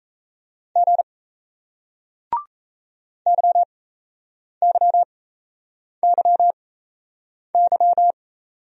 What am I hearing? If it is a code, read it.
Morse code, heard as a keyed tone, about 34 wpm, spaced as for 8 wpm, 700 Hz. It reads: G YYYY
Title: Speed-Racing Single Letters 28wpm